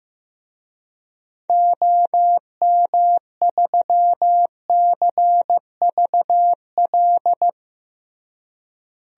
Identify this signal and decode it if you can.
Morse code, heard as a keyed tone, about 15 wpm, 700 Hz. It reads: OM3CVL
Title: DXCC-100 Call Signs - 6 Characters - Repeated 3x - 15wpm